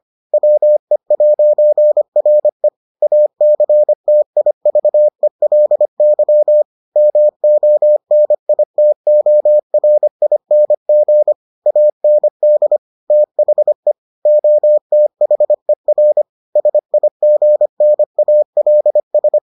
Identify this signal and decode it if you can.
Morse code, heard as a keyed tone, about 25 wpm, 600 Hz. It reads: WE'RE ACTIVELY MONITORING AND THE OTHER SIGNALS